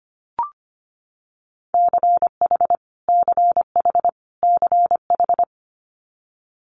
Morse code, heard as a keyed tone, about 25 wpm, 700 Hz. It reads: CH CH CH
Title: Multi-letter Phonemes Repeated 3x 25wpm